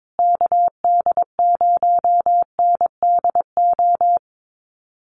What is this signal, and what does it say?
Morse code, heard as a keyed tone, about 22 wpm, 700 Hz. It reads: KD0NDO
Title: USA Call Signs Repeated 3x 22wpm